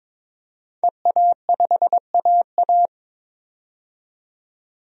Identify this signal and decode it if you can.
Morse code, heard as a keyed tone, about 22 wpm, 700 Hz. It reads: EA5AA
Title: DXCC-100 Call Signs - 5 Characters - 22wpm